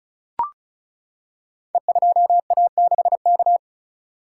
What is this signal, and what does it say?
Morse code, heard as a keyed tone, about 36 wpm, 700 Hz. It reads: E2A6K
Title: Code Groups - 5 Letter:Numbers - 35wpm